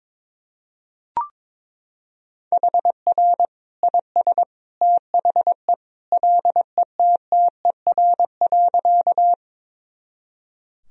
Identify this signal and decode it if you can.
Morse code, heard as a keyed tone, about 22 wpm, 700 Hz. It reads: HR IS THE LETTER.
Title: Sentences from Top 300 Words and Abbreviations 22wpm